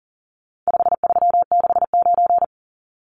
Morse code, heard as a keyed tone, about 40 wpm, 700 Hz. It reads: 5369
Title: Numbers - 4-Digit 40wpm